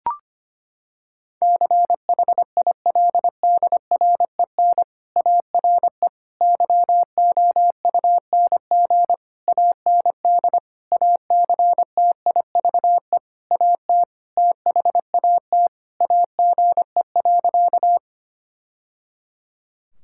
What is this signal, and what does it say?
Morse code, heard as a keyed tone, about 25 wpm, 700 Hz. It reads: CHILDREN ARE YOUNG AND ACTIVE AT THAT AGE.
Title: Sentences from Top 1000 Words - Review Part 6 - 25wpm